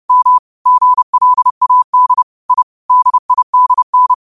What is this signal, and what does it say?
Morse code, heard as a keyed tone, about 30 wpm, 1000 Hz. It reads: M GLAD I DIDN